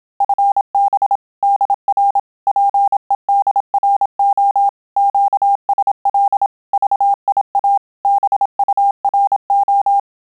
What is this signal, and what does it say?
Morse code, heard as a keyed tone, about 27 wpm, 800 Hz. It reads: FB DR PEDRO QSL VIA BURO
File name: Telegraphy-group / QSO CW 25 PPM